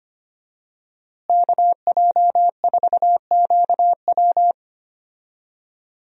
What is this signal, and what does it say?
Morse code, heard as a keyed tone, about 25 wpm, 700 Hz. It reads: KJ4QW
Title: USA Call Signs - 5 Characters - Repeated 3x - 25wpm